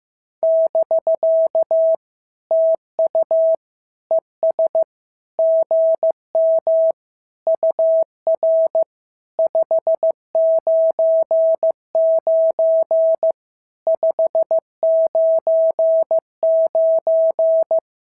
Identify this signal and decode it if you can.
Morse code, heard as a keyed tone, about 15 wpm, 650 Hz. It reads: <BK> TU ES GM UR 599 599
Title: SOTA QSO Practice 15wpm